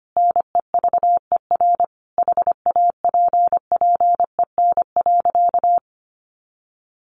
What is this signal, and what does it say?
Morse code, heard as a keyed tone, about 25 wpm, 700 Hz. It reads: NEVER HAPPEN.
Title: Sentences from Top 1200 Words 25wpm